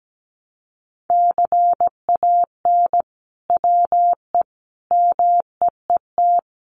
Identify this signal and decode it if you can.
Morse code, heard as a keyed tone, about 17 wpm, 700 Hz. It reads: CAN WE MEET
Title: Sentences from Top 1100 Words 17wpm